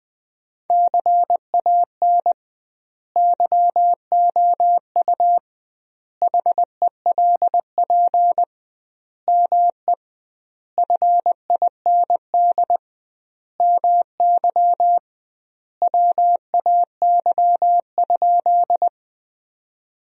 Morse code, heard as a keyed tone, about 20 wpm, 700 Hz. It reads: CAN YOU HELP ME FIND MY WAY?
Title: 2x Word-Spacing - Sentences from Top 200 Words - 20wpm